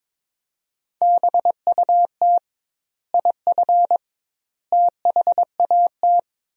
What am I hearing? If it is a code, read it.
Morse code, heard as a keyed tone, about 22 wpm, 700 Hz. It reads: BUT IF THAT
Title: 2x Word-Spacing - Sentences from Top 200 Words - 22wpm